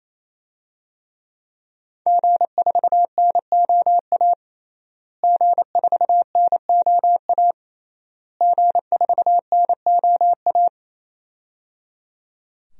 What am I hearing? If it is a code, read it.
Morse code, heard as a keyed tone, about 28 wpm, 700 Hz. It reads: G4NOA G4NOA G4NOA
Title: DXCC-100 Call Signs - 5 Characters - Repeated 3x - 28wpm